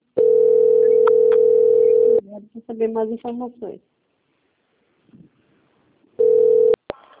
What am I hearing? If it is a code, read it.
Ringback tone (North America), 440 + 480 Hz, with speech.
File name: out- - - - - .17848